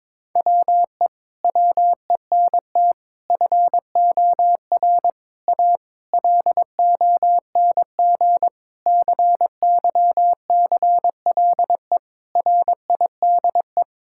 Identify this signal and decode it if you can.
Morse code, heard as a keyed tone, about 22 wpm, 700 Hz. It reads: WE WENT FOR A LONG CYCLE RIDE
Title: Sentences from Top 1000 Words - Review Part 6 - 22wpm